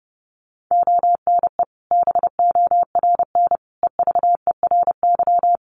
Morse code, heard as a keyed tone, about 30 wpm, 700 Hz. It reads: ONE BORN EVERY
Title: Sentences from Top 1400 Words 30wpm